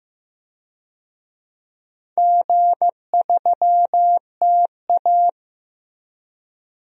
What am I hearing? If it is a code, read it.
Morse code, heard as a keyed tone, about 15 wpm, 700 Hz. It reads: G3TA
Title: DXCC-100 Call Signs - 4 Characters - Repeated 3x - 15wpm